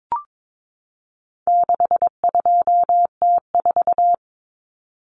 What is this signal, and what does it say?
Morse code, heard as a keyed tone, about 22 wpm, 700 Hz. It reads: 62T4
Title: Common Cut Numbers - 4-Digit 22wpm